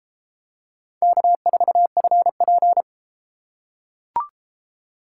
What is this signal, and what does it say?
Morse code, heard as a keyed tone, about 30 wpm, 700 Hz. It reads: K4FP